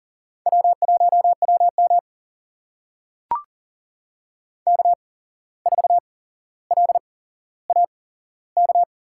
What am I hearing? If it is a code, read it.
Morse code, heard as a keyed tone, about 40 wpm, spaced as for 16 wpm, 700 Hz. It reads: W1WM K4LAK
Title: USA Call Signs - 8x Character Spacing - 40wpm